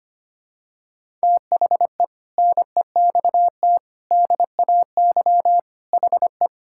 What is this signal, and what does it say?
Morse code, heard as a keyed tone, about 25 wpm, 700 Hz. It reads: THE NEXT DAY HE
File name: practice / 25wpm / Sentences with 3-Word Phrases - 25wpm